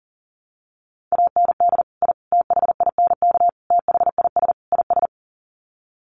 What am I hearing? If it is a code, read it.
Morse code, heard as a keyed tone, about 40 wpm, 700 Hz. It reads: AND I THINK THIS IS
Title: Sets of 5 Words Encore 40wpm